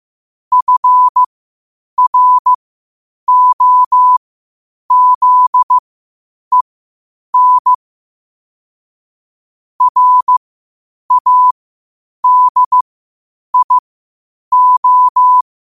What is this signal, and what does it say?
Morse code, heard as a keyed tone, about 15 wpm, spaced as for 8 wpm, 1000 Hz. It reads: FROZEN RADIO
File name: qotdfiles / 8wpm